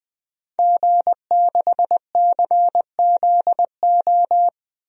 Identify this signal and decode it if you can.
Morse code, heard as a keyed tone, about 20 wpm, 700 Hz. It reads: G6CZO
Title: DXCC-100 Call Signs - 5 Characters - 20wpm